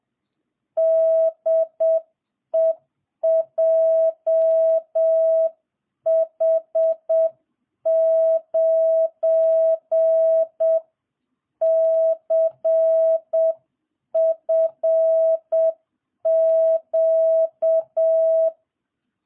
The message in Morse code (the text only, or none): DE JH9CFQ